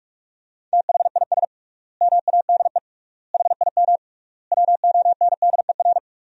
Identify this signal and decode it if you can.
Morse code, heard as a keyed tone, about 46 wpm, 700 Hz. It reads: THIS MADE HIM WONDER
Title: 3x Word-Spacing - Sentences from Top 500 Words - 45wpm